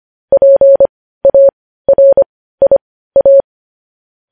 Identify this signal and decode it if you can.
Morse code, heard as a keyed tone, about 25 wpm, spaced as for 15 wpm, 550 Hz. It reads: PARIA
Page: news headlines in morse code at 15 wpm